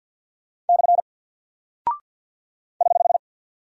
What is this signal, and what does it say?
Morse code, heard as a keyed tone, about 48 wpm, 700 Hz. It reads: / <HH>
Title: Single Character (Letter:Number:Punctuation:Prosign) Alt 50wpm